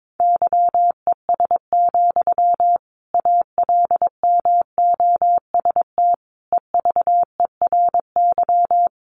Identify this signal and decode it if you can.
Morse code, heard as a keyed tone, about 22 wpm, 700 Hz. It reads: YES, ALMOST EVERY
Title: Sentences from Top 700 Words 22wpm